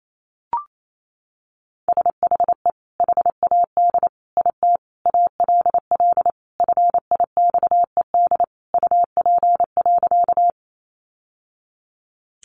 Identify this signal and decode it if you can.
Morse code, heard as a keyed tone, about 28 wpm, 700 Hz. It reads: SHE HAD IT ALL FIXED UP.
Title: Sentences from Top 1100 Words 28wpm